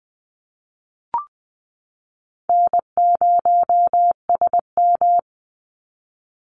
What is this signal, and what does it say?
Morse code, heard as a keyed tone, about 20 wpm, 700 Hz. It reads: N0SM
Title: Easy Call Signs Repeated 3x - 20wpm